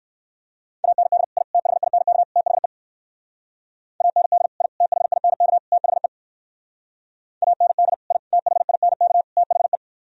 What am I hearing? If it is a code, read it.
Morse code, heard as a keyed tone, about 58 wpm, 700 Hz. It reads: AND I THINK THE AND I THINK THE AND I THINK THE